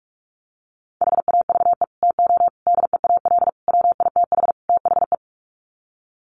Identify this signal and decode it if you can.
Morse code, heard as a keyed tone, about 44 wpm, 700 Hz. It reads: HAVE TO DEAL WITH THE
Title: Sets of 5 Words Encore2 45wpm